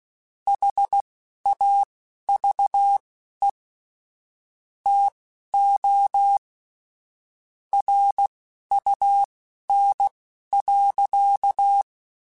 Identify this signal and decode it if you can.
Morse code, heard as a keyed tone, about 16 wpm, spaced as for 11 wpm, 800 Hz. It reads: HAVE TO RUN.